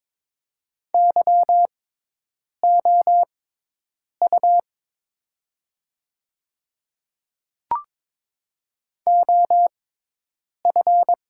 Morse code, heard as a keyed tone, about 22 wpm, spaced as for 8 wpm, 700 Hz. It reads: YOU OF
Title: 6x Character Spacing - Top 200 Words - 22wpm